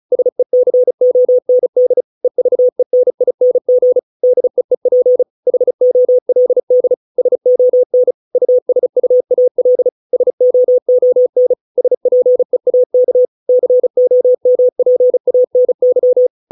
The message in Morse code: SECOND EVENING DEEP HOLD SON USUAL SOON SPEAK COMPANY

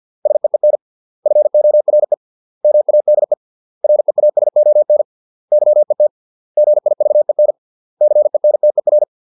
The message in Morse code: SEEN FORE MADE REASON YET GIVEN CENTER

